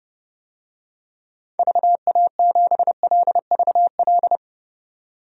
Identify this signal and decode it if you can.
Morse code, heard as a keyed tone, about 32 wpm, 700 Hz. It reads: VA7LVL
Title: DXCC-100 Call Signs - 6 Characters - 30wpm